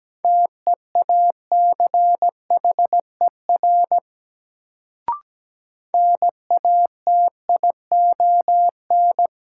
Morse code, heard as a keyed tone, about 17 wpm, 700 Hz. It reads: TEACHER NATION
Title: Sentences from Top 900 Words - New Word Drill - 17wpm